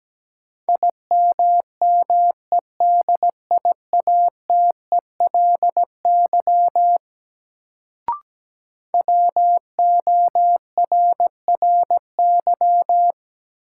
Morse code, heard as a keyed tone, about 17 wpm, 700 Hz. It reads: IMMEDIATELY WORRY
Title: Sentences from Top 1100 Words - New Word Drill - 17wpm